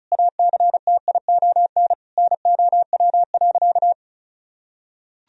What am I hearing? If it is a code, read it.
Morse code, heard as a keyed tone, about 36 wpm, 700 Hz. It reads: ACTION NOW.
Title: Sentences from Top 1000 Words - Review Part 4 - 35wpm